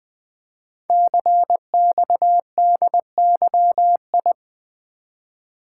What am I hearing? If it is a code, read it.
Morse code, heard as a keyed tone, about 20 wpm, 700 Hz. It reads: CXDYI
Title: Code Groups - 5 Letters - 20wpm